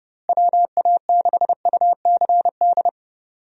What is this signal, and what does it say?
Morse code, heard as a keyed tone, about 32 wpm, 700 Hz. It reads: WA6UCD
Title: USA Call Signs - 6 Characters - Repeated 3x - 30wpm